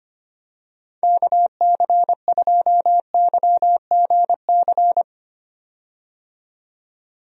Morse code, heard as a keyed tone, about 25 wpm, 700 Hz. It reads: KC2YGC
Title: USA Call Signs - 6 Characters - 25wpm